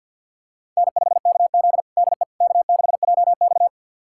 Morse code, heard as a keyed tone, about 47 wpm, 700 Hz. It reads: N5KZ DE K6JX
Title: USA Callsign DE Callsign - 4 Characters - 50wpm